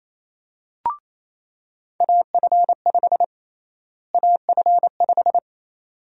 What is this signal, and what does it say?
Morse code, heard as a keyed tone, about 28 wpm, 700 Hz. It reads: AF5 AF5
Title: USA Call Sign Prefixes - Repeated 2x - 28wpm